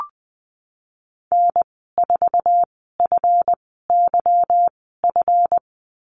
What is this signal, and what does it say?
Morse code, heard as a keyed tone, about 20 wpm, spaced as for 15 wpm, 700 Hz. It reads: N4FYF
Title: 207 Lesson 33g - Letter j - Call Signs 20wpm